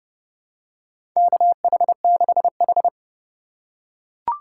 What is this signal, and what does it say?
Morse code, heard as a keyed tone, about 30 wpm, 700 Hz. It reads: KH6H
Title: USA Call Signs - 2x Character Spacing - 30wpm